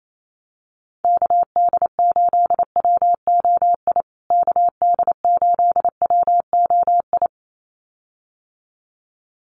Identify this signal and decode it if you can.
Morse code, heard as a keyed tone, about 28 wpm, 700 Hz. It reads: KD8WOI KD8WOI
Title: USA Call Signs Repeated 2x 28wpm